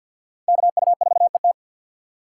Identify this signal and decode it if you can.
Morse code, heard as a keyed tone, about 53 wpm, 700 Hz. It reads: KU4ET
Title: USA Call Signs - 5 Characters - 2x Character Spacing - 50wpm